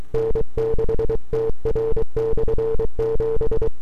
Morse code, heard as a keyed tone, about 23 wpm, 450 Hz. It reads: N6TR/7
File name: n6tr / 12Oct2009-160